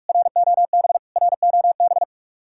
Morse code, heard as a keyed tone, about 59 wpm, spaced as for 52 wpm, 700 Hz. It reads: ROB ROB
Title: Speed-Racing Common QSO Phrases - 45wpm